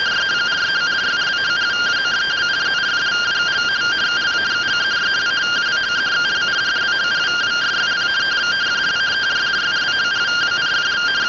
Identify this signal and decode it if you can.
Two-tone teleprinter signal, about 170 Hz shift.